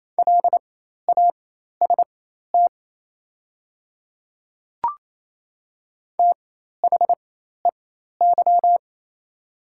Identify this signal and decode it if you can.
Morse code, heard as a keyed tone, about 28 wpm, spaced as for 13 wpm, 700 Hz. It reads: LAST THEY
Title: 4x Character Spacing - Top 500 Words - 28wpm